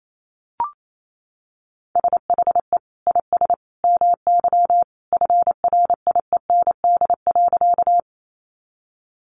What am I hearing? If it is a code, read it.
Morse code, heard as a keyed tone, about 28 wpm, 700 Hz. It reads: SHE IS MY FRIEND.
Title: Sentences from Top 300 Words 28wpm